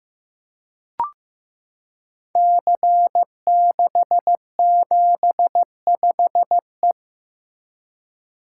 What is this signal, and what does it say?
Morse code, heard as a keyed tone, about 15 wpm, 700 Hz. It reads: C675E
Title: Code Groups - 5 Letter:Numbers - 15wpm